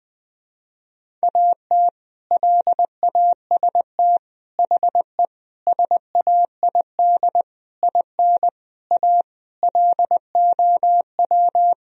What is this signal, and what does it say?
Morse code, heard as a keyed tone, about 21 wpm, 700 Hz. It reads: AT LAST HE SAID IN A LOW
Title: Sentences from Top 1000 Words - Review Part 1 - 20wpm